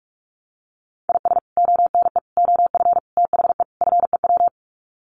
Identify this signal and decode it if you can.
Morse code, heard as a keyed tone, about 44 wpm, 700 Hz. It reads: IS ONE OF THE FEW
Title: Sets of 5 Words 45wpm